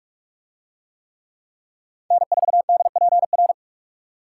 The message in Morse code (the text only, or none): N4DPR